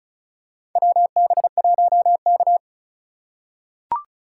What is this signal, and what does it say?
Morse code, heard as a keyed tone, about 36 wpm, 700 Hz. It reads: WB1K
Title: USA Call Signs - 4 Characters - 3x Character Spacing - 35wpm